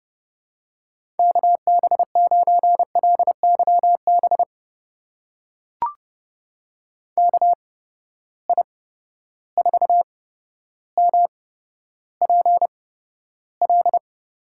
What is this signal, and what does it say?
Morse code, heard as a keyed tone, about 30 wpm, spaced as for 13 wpm, 700 Hz. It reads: KB9LYB KI4MPL
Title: USA Call Signs - 6 Characters - 8x Character Spacing - 30wpm